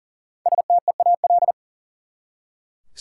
Morse code, heard as a keyed tone, about 40 wpm, 700 Hz. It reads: STEAL
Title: Five Letter Words - 40wpm